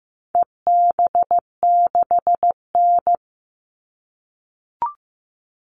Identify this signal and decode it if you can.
Morse code, heard as a keyed tone, about 15 wpm, 700 Hz. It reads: EB6N
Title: All Cut Numbers - 4-Digit 15wpm